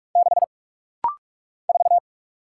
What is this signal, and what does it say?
Morse code, heard as a keyed tone, about 46 wpm, 700 Hz. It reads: 6 4
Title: Single Letter-Number - Rapid-Fire 45wpm